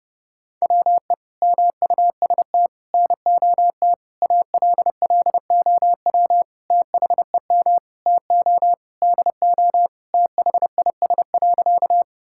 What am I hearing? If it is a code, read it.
Morse code, heard as a keyed tone, about 30 wpm, 700 Hz. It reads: WE MUST NOT ALLOW THEM TO DO THIS.
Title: Sentences from Top 1000 Words - Review Part 5 - 30wpm